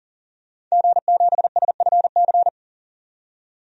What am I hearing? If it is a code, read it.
Morse code, heard as a keyed tone, about 37 wpm, 700 Hz. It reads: G7SFC
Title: DXCC-100 Call Signs - 5 Characters - 40wpm